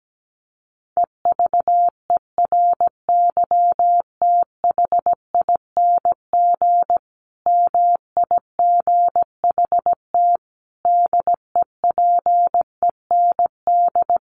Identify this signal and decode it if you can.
Morse code, heard as a keyed tone, about 17 wpm, 700 Hz. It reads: EVERYTHING MIGHT DEPEND
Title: Sentences from Top 1300 Words 17wpm